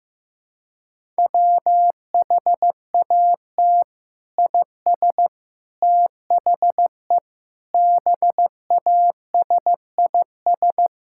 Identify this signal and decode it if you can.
Morse code, heard as a keyed tone, about 15 wpm, 700 Hz. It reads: WHAT IS THE BASIS
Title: Sentences from Top 1000 Words - Review Part 5 - 15wpm